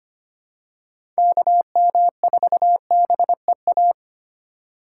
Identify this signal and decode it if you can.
Morse code, heard as a keyed tone, about 25 wpm, 700 Hz. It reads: KM4BEA
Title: USA Call Signs - 6 Characters - 8x Character Spacing - 25wpm